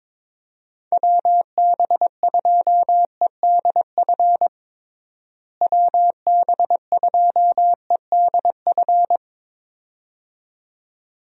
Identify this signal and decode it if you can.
Morse code, heard as a keyed tone, about 22 wpm, 700 Hz. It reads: WB2EDF WB2EDF
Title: USA Call Signs - 6 Characters - Repeated 2x - 22wpm